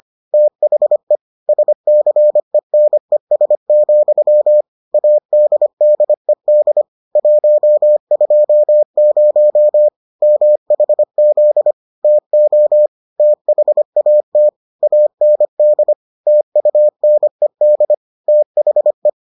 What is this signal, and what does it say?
Morse code, heard as a keyed tone, about 25 wpm, 600 Hz. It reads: THE SCENES, ADDED 120 MHZ TO THAT AND TUNED THE